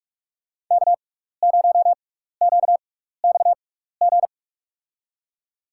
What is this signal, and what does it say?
Morse code, heard as a keyed tone, about 46 wpm, spaced as for 16 wpm, 690 Hz. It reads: K0QXG
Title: USA Call Signs - 5 Characters - 6x Character Spacing - 45wpm